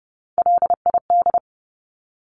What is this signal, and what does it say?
Morse code, heard as a keyed tone, about 30 wpm, 700 Hz. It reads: LID
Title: Abbreviations Repeated 3x - 30wpm